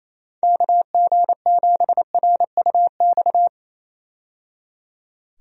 Morse code, heard as a keyed tone, about 28 wpm, 700 Hz. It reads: KG7RUX
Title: USA Call Signs - 6 Characters - 28wpm